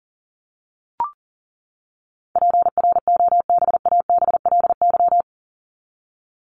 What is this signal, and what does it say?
Morse code, heard as a keyed tone, about 40 wpm, 700 Hz. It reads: PROBABLY
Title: Top 501-750 Words 40wpm